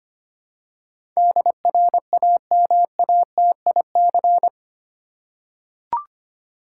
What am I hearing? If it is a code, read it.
Morse code, heard as a keyed tone, about 25 wpm, 700 Hz. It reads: DRAMATIC
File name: Eight Letter Words - 25wpm